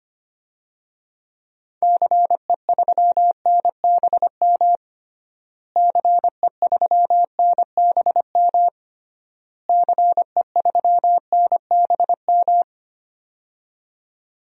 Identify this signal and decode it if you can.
Morse code, heard as a keyed tone, about 25 wpm, 700 Hz. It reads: CE3NBM CE3NBM CE3NBM
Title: DXCC-100 Call Signs - 6 Characters - Repeated 3x - 25wpm